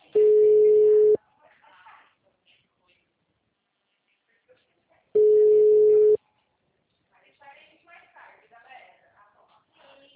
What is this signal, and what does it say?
Ringback tone (Europe, 425 Hz), with speech.
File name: out- -2006- - - .40